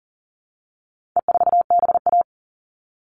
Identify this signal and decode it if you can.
Morse code, heard as a keyed tone, about 40 wpm, 700 Hz. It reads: E4BA
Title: All Cut Numbers - 4-Digit 40wpm